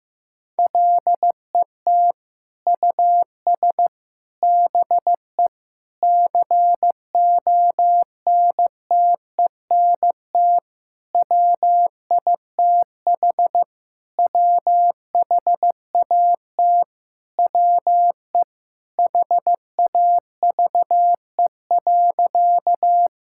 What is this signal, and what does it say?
Morse code, heard as a keyed tone, about 15 wpm, 700 Hz. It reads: LET US BE CONTENT WITH WHAT WE HAVE.
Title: Sentences from Top 1000 Words - Review Part 6 - 15wpm